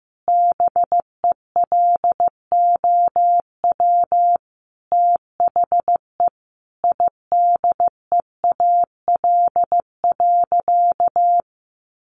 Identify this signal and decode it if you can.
Morse code, heard as a keyed tone, about 15 wpm, 700 Hz. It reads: BELOW THE IDEAL.
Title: Sentences from Top 1400 Words 15wpm